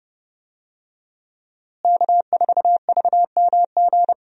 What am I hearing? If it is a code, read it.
Morse code, heard as a keyed tone, about 30 wpm, 700 Hz. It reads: K4VMG
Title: USA Call Signs - 5 Characters - Repeated 2x - 30wpm